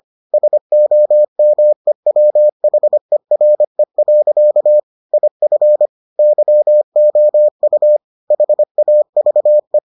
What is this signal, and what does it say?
Morse code, heard as a keyed tone, about 25 wpm, 600 Hz. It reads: SOMEWHERE. IF YOU HAVE